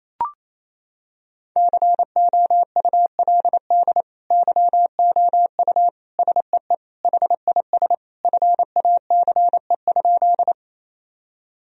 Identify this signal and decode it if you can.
Morse code, heard as a keyed tone, about 28 wpm, 700 Hz. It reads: COULD YOU SEE HIS FACE?
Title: Sentences from Top 1000 Words - Review Part 2 - 28wpm